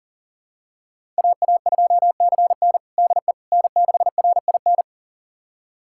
Morse code, heard as a keyed tone, about 40 wpm, 700 Hz. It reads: AA2CN DE N6RIN
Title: USA Callsign DE Callsign - 5 Characters - 40wpm